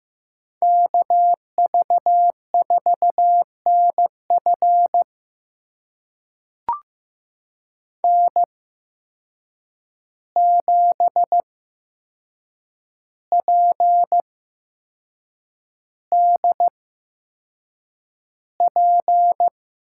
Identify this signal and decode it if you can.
Morse code, heard as a keyed tone, about 15 wpm, spaced as for 6 wpm, 700 Hz. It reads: KV4NF N7PDP
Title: USA Call Signs - 5 Characters - 8x Character Spacing - 15wpm